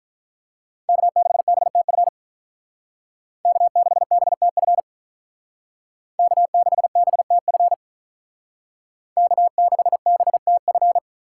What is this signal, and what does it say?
Morse code, heard as a keyed tone, about 45 wpm, 700 Hz. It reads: K6BTF K6BTF K6BTF K6BTF